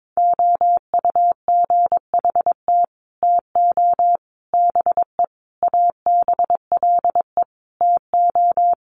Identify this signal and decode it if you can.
Morse code, heard as a keyed tone, about 22 wpm, 700 Hz. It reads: OUGHT TO BE ABLE TO
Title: Sets of 5 Words Encore2 22wpm